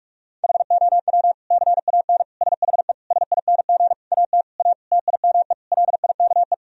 Morse code, heard as a keyed tone, about 46 wpm, 700 Hz. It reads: HOW CAN SHE SING AT A TIME LIKE
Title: Sentences from Top 1000 Words - Review Part 2 - 45wpm